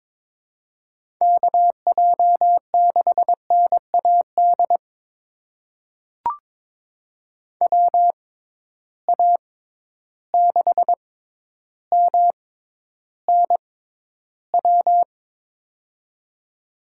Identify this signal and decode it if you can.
Morse code, heard as a keyed tone, about 21 wpm, spaced as for 11 wpm, 700 Hz. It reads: KJ6NAD WA6MNW
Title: USA Call Signs - 6 Characters - 6x Character Spacing - 22wpm